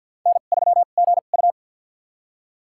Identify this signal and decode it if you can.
Morse code, heard as a keyed tone, about 47 wpm, spaced as for 35 wpm, 700 Hz. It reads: N3GU
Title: USA Call Signs - 4 Characters - 2x Character Spacing - 50wpm